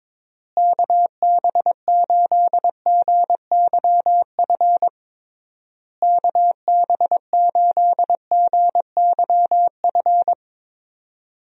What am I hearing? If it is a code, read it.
Morse code, heard as a keyed tone, about 22 wpm, 700 Hz. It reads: KB8GYF KB8GYF
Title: USA Call Signs - 6 Characters - Repeated 3x - 22wpm